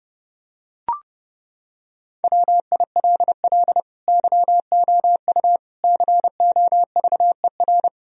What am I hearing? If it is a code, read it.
Morse code, heard as a keyed tone, about 30 wpm, 700 Hz. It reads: WILL YOU COVER
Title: Sentences from Top 1000 Words - Review Part 1 - 30wpm